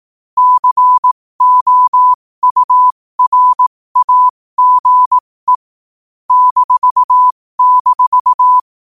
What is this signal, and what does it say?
Morse code, heard as a keyed tone, about 18 wpm, spaced as for 15 wpm, 1000 Hz. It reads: COURAGE --
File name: qotdfiles / 15wpm